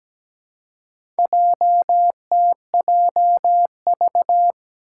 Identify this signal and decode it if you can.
Morse code, heard as a keyed tone, about 17 wpm, 700 Hz. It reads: JTJV
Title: Four Challenging Letter-Number Combinations 17wpm